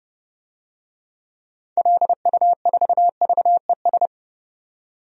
Morse code, heard as a keyed tone, about 30 wpm, 700 Hz. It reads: LU4VES
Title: DXCC-100 Call Signs - 6 Characters - 30wpm